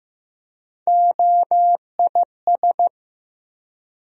Morse code, heard as a keyed tone, about 15 wpm, 700 Hz. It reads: OIS
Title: USA Call Sign Postfixes - 3 Letters 15wpm